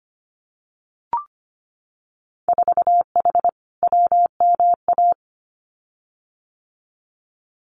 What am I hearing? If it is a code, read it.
Morse code, heard as a keyed tone, about 25 wpm, 700 Hz. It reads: 4H WMA
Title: Winter Field Day Exchanges Encore 25wpm